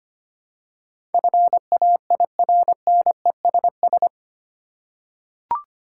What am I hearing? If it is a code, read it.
Morse code, heard as a keyed tone, about 25 wpm, 700 Hz. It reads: FAIRNESS
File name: Eight Letter Words - 25wpm